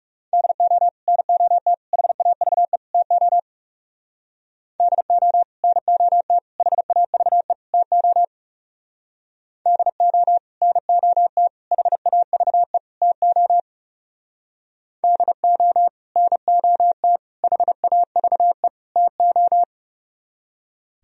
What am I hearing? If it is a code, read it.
Morse code, heard as a keyed tone, about 36 wpm, 700 Hz. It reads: DO NOT HAVE TO DO NOT HAVE TO DO NOT HAVE TO DO NOT HAVE TO
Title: Speed-Racing - Sets of 4 Words 30wpm